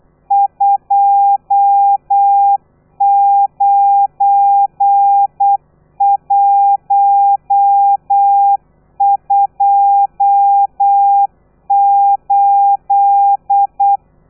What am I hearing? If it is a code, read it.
Morse code, heard as a keyed tone, about 8 wpm, 800 Hz. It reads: 29128